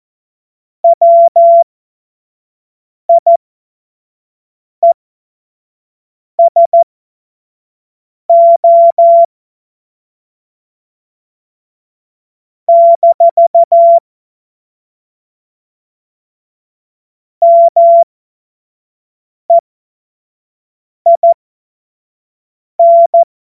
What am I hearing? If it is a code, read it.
Morse code, heard as a keyed tone, about 14 wpm, spaced as for 5 wpm, 680 Hz. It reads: WIESO - MEIN
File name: Morsekurs / Witze / Witz 15. August 2023 25 F70